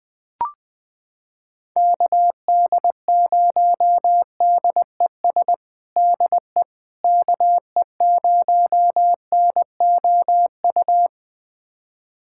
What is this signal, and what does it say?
Morse code, heard as a keyed tone, about 21 wpm, 700 Hz. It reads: KD0DES DE KE0NOU
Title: USA Callsign DE Callsign - 6 Characters - 20wpm